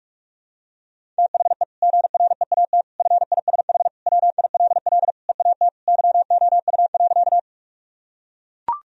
Morse code, heard as a keyed tone, about 41 wpm, 700 Hz. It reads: THE GREAT FISH WILL EAT YOU.
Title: Sentences from Top 1000 Words - Review Part 1 - 45wpm